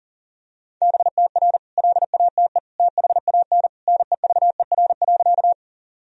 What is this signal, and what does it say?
Morse code, heard as a keyed tone, about 40 wpm, 700 Hz. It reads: BTR LATE THAN NEVER.
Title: Sentences from Top 400 Words and Abbreviations 40wpm